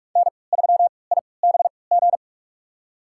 Morse code, heard as a keyed tone, about 47 wpm, spaced as for 26 wpm, 700 Hz. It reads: N3IBG
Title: USA Call Signs - 5 Characters - 3x Character Spacing - 45wpm